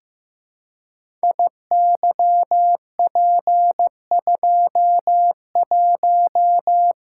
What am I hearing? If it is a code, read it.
Morse code, heard as a keyed tone, about 15 wpm, 700 Hz. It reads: IYP21
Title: Code Groups - 5 Letter:Numbers - 15wpm